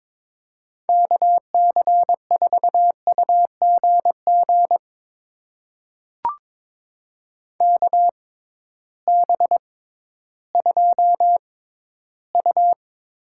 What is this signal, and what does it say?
Morse code, heard as a keyed tone, about 21 wpm, spaced as for 13 wpm, 700 Hz. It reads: KC4UGG KB2U
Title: USA Call Signs - 6 Characters - 6x Character Spacing - 22wpm